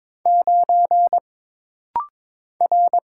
Morse code, heard as a keyed tone, about 22 wpm, 700 Hz. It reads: 9 R
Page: This rapid-fire timing is geared to push your ICR (Instant Character Recognition) skills.